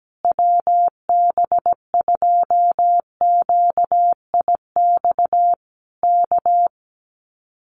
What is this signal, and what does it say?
Morse code, heard as a keyed tone, about 17 wpm, 700 Hz. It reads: WB2QIX K
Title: Common QSO Phrases Encore 17wpm